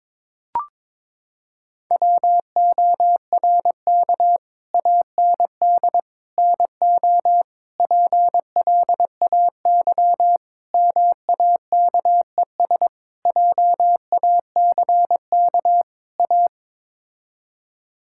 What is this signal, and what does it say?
Morse code, heard as a keyed tone, about 21 wpm, 700 Hz. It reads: WORK AND NO PLAY MAKES JACK A
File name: practice / 22wpm / Sets of 7 Words - 22wpm